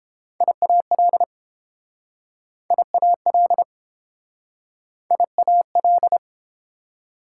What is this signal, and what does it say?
Morse code, heard as a keyed tone, about 30 wpm, 700 Hz. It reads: IAL IAL IAL